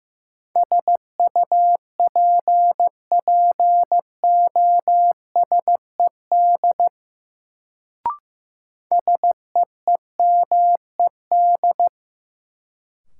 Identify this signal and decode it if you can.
Morse code, heard as a keyed tone, about 15 wpm, 700 Hz. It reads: SUPPOSED SEEMED
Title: Sets of 3 Words - New Word Drill - 15wpm